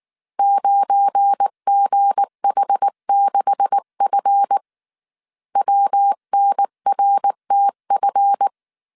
Morse code, heard as a keyed tone, about 19 wpm, 800 Hz. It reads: 9GH6F WNRTF
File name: CW / DET / 1 9 1500C S B 18wpm 2551